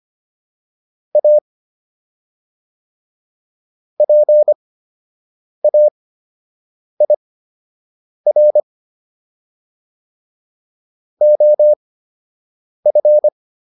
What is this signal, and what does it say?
Morse code, heard as a keyed tone, about 25 wpm, spaced as for 7 wpm, 600 Hz. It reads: A PAIR OF